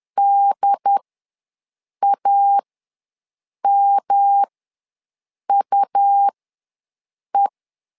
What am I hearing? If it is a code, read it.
Morse code, heard as a keyed tone, about 10 wpm, spaced as for 6 wpm, 800 Hz. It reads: DAMUE
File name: LEZIONI / 3 MWRO / 3 M ws 10wpm 2121